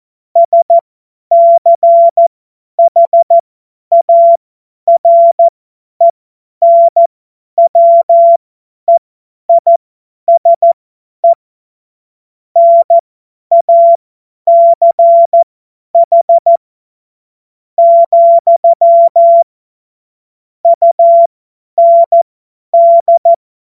Morse code, heard as a keyed tone, about 14 wpm, spaced as for 10 wpm, 680 Hz. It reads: SCHARENWEISE NACH, UND
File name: Morsekurs / Witze / Witz 30. August 2022 50 F70